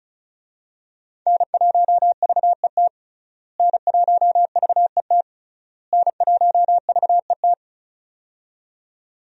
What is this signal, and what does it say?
Morse code, heard as a keyed tone, about 36 wpm, 700 Hz. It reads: N1VET N1VET N1VET
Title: USA Call Signs - 5 Characters - Repeated 3x - 35wpm